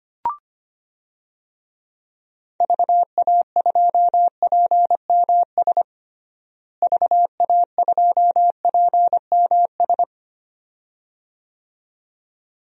Morse code, heard as a keyed tone, about 25 wpm, 700 Hz. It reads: VA2PMS VA2PMS
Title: DXCC-100 Call Signs - 6 Characters - Repeated 2x - 25wpm